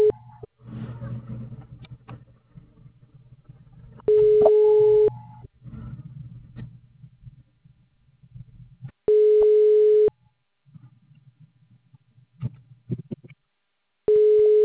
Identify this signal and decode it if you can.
Ringback tone (Europe, 425 Hz).
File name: out- -2001- - - .0